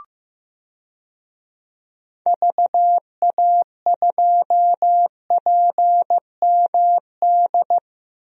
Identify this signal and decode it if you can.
Morse code, heard as a keyed tone, about 15 wpm, 700 Hz. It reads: VA2PMD
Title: DXCC-100 Call Signs - 6 Characters - Repeated 3x - 15wpm